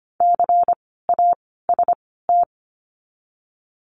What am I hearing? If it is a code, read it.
Morse code, heard as a keyed tone, about 25 wpm, 700 Hz. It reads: C A S T
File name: course / 25:10wpm / 094 Lesson 16f - Letter c - Words 25wpm